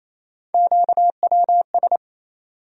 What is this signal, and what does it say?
Morse code, heard as a keyed tone, about 28 wpm, 700 Hz. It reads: QWS
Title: USA Call Sign Postfixes - 3 Letters 28wpm